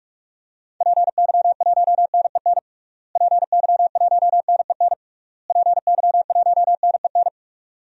Morse code, heard as a keyed tone, about 46 wpm, 700 Hz. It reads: PY1NEN PY1NEN PY1NEN
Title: DXCC-100 Call Signs - 6 Characters - Repeated 3x - 45wpm